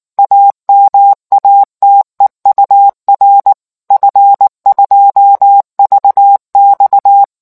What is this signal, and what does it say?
Morse code, heard as a keyed tone, about 19 wpm, 800 Hz. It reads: AMATEUR F2VX